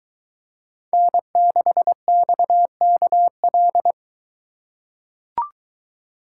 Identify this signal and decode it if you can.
Morse code, heard as a keyed tone, about 22 wpm, 700 Hz. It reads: N6XKL